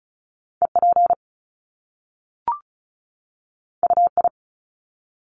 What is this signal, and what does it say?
Morse code, heard as a keyed tone, about 36 wpm, 700 Hz. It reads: EP UI
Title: Top Two Letter Combinations in Words 35wpm